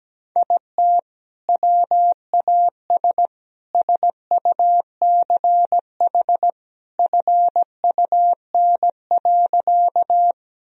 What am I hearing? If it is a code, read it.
Morse code, heard as a keyed tone, about 17 wpm, 700 Hz. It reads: IT WAS SUCH FUN.
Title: Sentences from Top 1000 Words - Review Part 4 - 17wpm